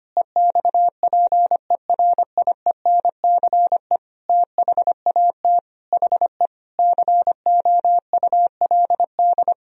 Morse code, heard as a keyed tone, about 25 wpm, 700 Hz. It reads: EXPERIENCE THAT HE COULD
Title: Sentences from Top 1000 Words - Review Part 3 - 25wpm